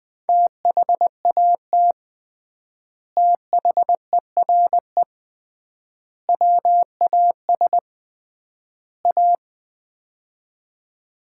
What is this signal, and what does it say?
Morse code, heard as a keyed tone, about 20 wpm, 700 Hz. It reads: THAT THERE WAS A
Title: 3x Word-Spacing - Sets of 4 Words - 20wpm